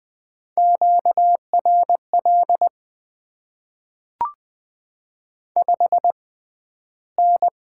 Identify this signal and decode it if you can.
Morse code, heard as a keyed tone, about 20 wpm, spaced as for 12 wpm, 690 Hz. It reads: QRL 5N
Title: Top 20 QSO Elements - 6x Character Spacing - 20wpm